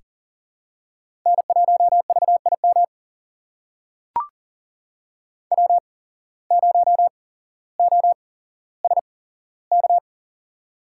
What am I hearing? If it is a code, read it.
Morse code, heard as a keyed tone, about 40 wpm, spaced as for 17 wpm, 700 Hz. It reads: N1VIM W0OSK
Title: USA Call Signs - 5 Characters - 8x Character Spacing - 40wpm